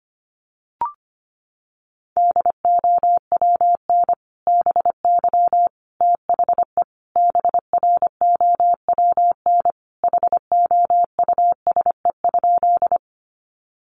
Morse code, heard as a keyed tone, about 25 wpm, 700 Hz. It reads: DOWN BY THE BROWN HOUSE?
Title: Sentences from Top 1200 Words 25wpm